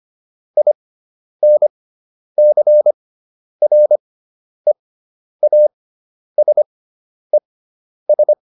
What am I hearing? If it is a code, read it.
Morse code, heard as a keyed tone, about 25 wpm, spaced as for 10 wpm, 600 Hz. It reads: INCREASES